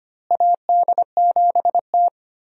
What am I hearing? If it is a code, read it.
Morse code, heard as a keyed tone, about 25 wpm, 700 Hz. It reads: AD7T
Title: USA Call Signs - 4 Characters - Repeated 2x - 25wpm